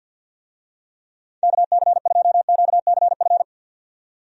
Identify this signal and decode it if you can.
Morse code, heard as a keyed tone, about 53 wpm, 700 Hz. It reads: KC2QCF
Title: USA Call Signs - 6 Characters - 50wpm